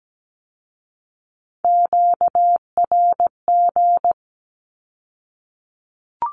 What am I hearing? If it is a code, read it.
Morse code, heard as a keyed tone, about 17 wpm, 700 Hz. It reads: QRG
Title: Q-Codes with Spoken Meaning - 17wpm